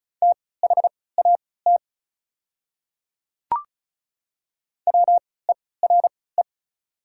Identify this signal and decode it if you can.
Morse code, heard as a keyed tone, about 36 wpm, spaced as for 20 wpm, 690 Hz. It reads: THAT WERE
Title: 3x Character Spacing - Top 100 Words - 35wpm